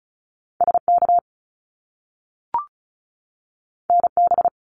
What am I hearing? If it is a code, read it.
Morse code, heard as a keyed tone, about 36 wpm, 700 Hz. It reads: SK NB
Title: Canadian Provinces and Territories 35wpm